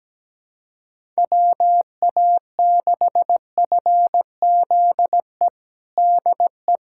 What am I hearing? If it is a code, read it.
Morse code, heard as a keyed tone, about 17 wpm, 700 Hz. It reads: WA6FZE DE
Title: USA Callsign DE Callsign - 6 Characters - 17wpm